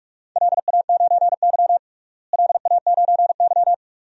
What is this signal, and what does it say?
Morse code, heard as a keyed tone, about 46 wpm, 700 Hz. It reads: LA9Y LA9Y
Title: DXCC-100 Call Signs - 4 Characters - Repeated 2x - 45wpm